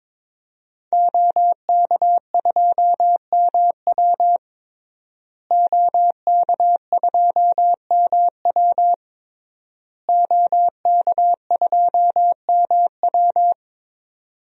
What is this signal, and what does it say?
Morse code, heard as a keyed tone, about 22 wpm, 700 Hz. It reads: OK2MW OK2MW OK2MW
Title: DXCC-100 Call Signs - 5 Characters - Repeated 3x - 22wpm